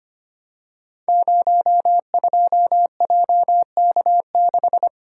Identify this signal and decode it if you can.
Morse code, heard as a keyed tone, about 25 wpm, 700 Hz. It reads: 02JK6
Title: Code Groups - 5 Letter:Numbers - 25wpm